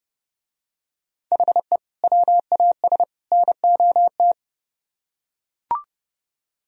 Morse code, heard as a keyed tone, about 28 wpm, 700 Hz. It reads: HE WAS NOT